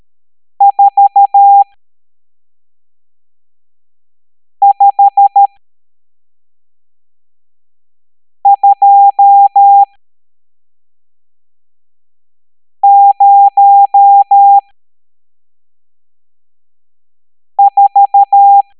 Morse code, heard as a keyed tone, about 13 wpm, spaced as for 3 wpm, 800 Hz. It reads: 45204
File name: hamdata / L12-EX02-CHIFFRES